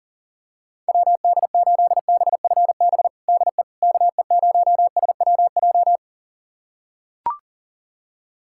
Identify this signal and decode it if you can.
Morse code, heard as a keyed tone, about 40 wpm, 700 Hz. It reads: WD8BFB DE KE0SWJ
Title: USA Callsign DE Callsign - 6 Characters - 40wpm